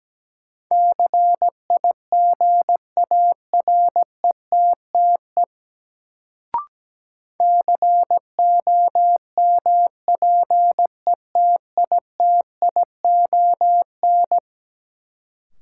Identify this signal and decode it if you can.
Morse code, heard as a keyed tone, about 17 wpm, 700 Hz. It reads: CIGARETTE COMPETITION
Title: Sentences from Top 1000 Words - New Word Drill - 17wpm